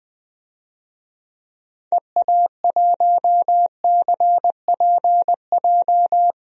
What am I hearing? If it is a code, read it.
Morse code, heard as a keyed tone, about 20 wpm, 700 Hz. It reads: EA1CPJ
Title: DXCC-100 Call Signs - 6 Characters - 20wpm